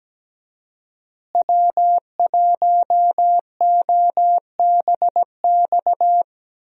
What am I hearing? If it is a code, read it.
Morse code, heard as a keyed tone, about 17 wpm, 700 Hz. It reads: W1OBX
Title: USA Call Signs - 5 Characters - Repeated 3x - 17wpm